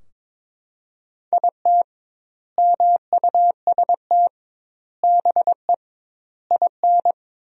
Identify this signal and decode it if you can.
Morse code, heard as a keyed tone, about 22 wpm, 700 Hz. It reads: IT MUST BE IN
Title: 2x Word-Spacing - Sentences from Top 300 Words - 22wpm